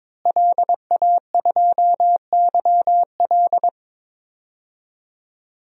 Morse code, heard as a keyed tone, about 22 wpm, 700 Hz. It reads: LA2YL
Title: DXCC-100 Call Signs - 5 Characters - Repeated 2x - 22wpm